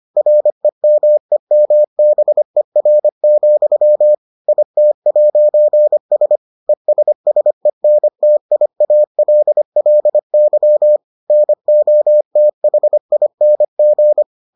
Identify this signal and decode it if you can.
Morse code, heard as a keyed tone, about 25 wpm, 600 Hz. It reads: REMEMBER, IT'S ESSENTIALLY NOTHING